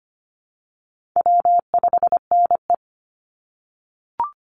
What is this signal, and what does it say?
Morse code, heard as a keyed tone, about 25 wpm, 700 Hz. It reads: W5NE
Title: Call Signs - 25wpm